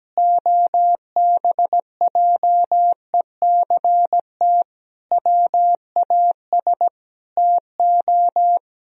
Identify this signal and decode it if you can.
Morse code, heard as a keyed tone, about 17 wpm, 700 Hz. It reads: OBJECT WAS TO
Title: Sentences from Top 1000 Words - Review Part 3 - 17wpm